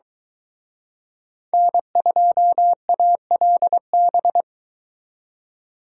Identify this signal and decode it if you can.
Morse code, heard as a keyed tone, about 23 wpm, 700 Hz. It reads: N2ALB